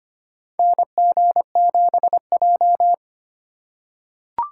Morse code, heard as a keyed tone, about 25 wpm, 700 Hz. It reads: NG7J
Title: USA Call Signs - 4 Characters - 8x Character Spacing - 25wpm